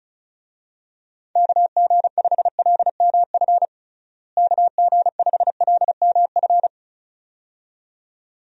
Morse code, heard as a keyed tone, about 34 wpm, 700 Hz. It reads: KG5LMF KG5LMF
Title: USA Call Signs - 6 Characters - Repeated 2x - 35wpm